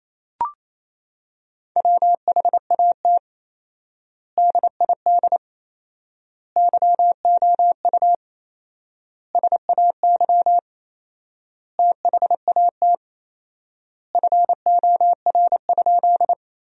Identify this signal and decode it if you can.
Morse code, heard as a keyed tone, about 28 wpm, 700 Hz. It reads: WHAT DID YOU SAY THAT FOR?
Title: 4x Word-Spacing - Sentences from Top 200 Words - 28wpm